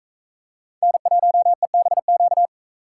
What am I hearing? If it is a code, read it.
Morse code, heard as a keyed tone, about 40 wpm, 700 Hz. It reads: N1EBQ